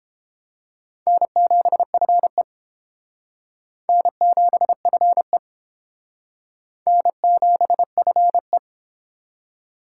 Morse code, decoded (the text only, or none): N7FE N7FE N7FE